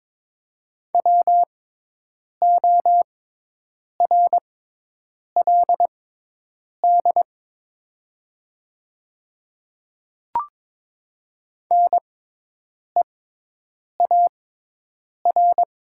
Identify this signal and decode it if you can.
Morse code, heard as a keyed tone, about 22 wpm, spaced as for 8 wpm, 700 Hz. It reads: WORLD NEAR
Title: 6x Character Spacing - Top 500 Words - 22wpm